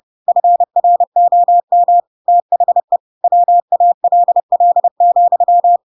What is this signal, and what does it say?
Morse code, heard as a keyed tone, about 30 wpm, 700 Hz. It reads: FROM THE WALL,
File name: Pinocchio / Pinocchio ch4 30 30 ik1hge